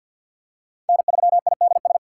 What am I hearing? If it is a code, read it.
Morse code, heard as a keyed tone, about 53 wpm, 700 Hz. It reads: N3IDS
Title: USA Call Signs - 2x Character Spacing - 50wpm